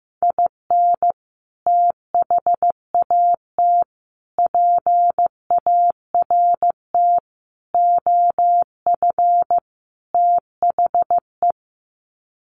Morse code, heard as a keyed tone, about 15 wpm, 700 Hz. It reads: IN THAT PART OF THE
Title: Sets of 5 Words Encore2 15wpm